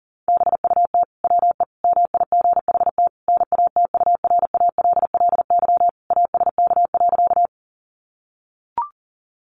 Morse code, heard as a keyed tone, about 40 wpm, 700 Hz. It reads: BUT WE MIGHT NATURALLY ASK.
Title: Sentences from Top 1300 Words 40wpm